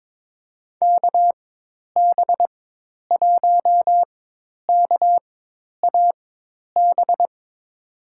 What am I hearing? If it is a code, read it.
Morse code, heard as a keyed tone, about 22 wpm, spaced as for 10 wpm, 700 Hz. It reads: KB1KAB
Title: USA Call Signs - 6 Characters - 4x Character Spacing - 22wpm